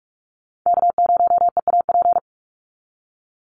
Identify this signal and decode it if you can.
Morse code, heard as a keyed tone, about 44 wpm, 700 Hz. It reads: K0EAP
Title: NA SOTA Chaser Callsigns 45wpm